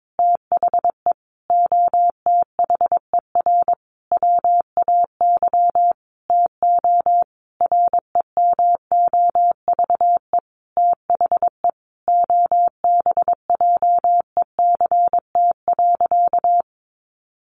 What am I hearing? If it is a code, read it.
Morse code, heard as a keyed tone, about 22 wpm, 700 Hz. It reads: THE OTHER WAY TO REMOVE THE OBJECT.
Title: Sentences from Top 800 Words 22wpm